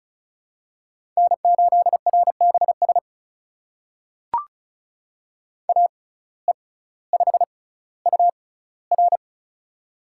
Morse code, heard as a keyed tone, about 36 wpm, spaced as for 18 wpm, 700 Hz. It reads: N8RBS AE5UR
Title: USA Call Signs - 5 Characters - 6x Character Spacing - 35wpm